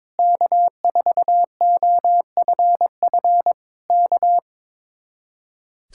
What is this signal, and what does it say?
Morse code, heard as a keyed tone, about 22 wpm, 700 Hz. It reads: K4OFF K
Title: Speed-Racing Common QSO Phrases Encore - 22wpm